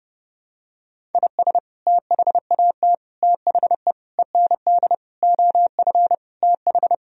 Morse code, heard as a keyed tone, about 32 wpm, 700 Hz. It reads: IS THAT THE END OF TH
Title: Sentences from Top 1000 Words - Review Part 1 - 30wpm